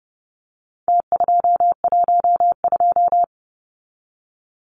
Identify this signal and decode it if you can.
Morse code, heard as a keyed tone, about 30 wpm, 700 Hz. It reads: T212
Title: Numbers - Contest Style - 4-Digit 30wpm